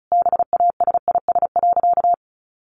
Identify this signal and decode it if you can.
Morse code, heard as a keyed tone, about 34 wpm, 700 Hz. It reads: BASIS.
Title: Sentences from Top 1500 Words 35wpm